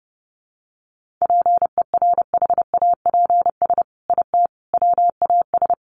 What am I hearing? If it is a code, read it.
Morse code, heard as a keyed tone, about 30 wpm, 700 Hz. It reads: PERHAPS IT WAS
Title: Sentences from Top 1400 Words 30wpm